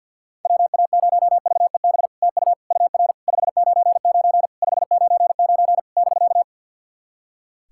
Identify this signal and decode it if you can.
Morse code, heard as a keyed tone, about 49 wpm, 700 Hz. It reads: WA0VEB TU UR 599 599 <BK>
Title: SOTA QSO Practice 50wpm